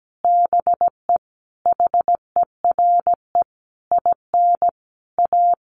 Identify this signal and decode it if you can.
Morse code, heard as a keyed tone, about 17 wpm, 700 Hz. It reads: BE HERE IN A
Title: Sentences from Top 1000 Words 17wpm